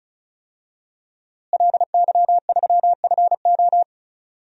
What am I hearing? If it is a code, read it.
Morse code, heard as a keyed tone, about 36 wpm, 700 Hz. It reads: LY3FO
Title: DXCC-100 Call Signs - 5 Characters - Repeated 2x - 35wpm